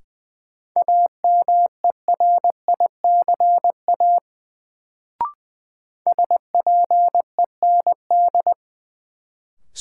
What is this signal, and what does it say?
Morse code, heard as a keyed tone, about 20 wpm, 700 Hz. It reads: AMERICA SPEND
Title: Sets of 5 Words - New Word Drill - 20wpm